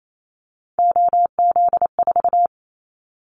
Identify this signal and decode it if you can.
Morse code, heard as a keyed tone, about 28 wpm, 700 Hz. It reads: OZ4
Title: Callsign Prefixes for Top 100 DXCC - 28wpm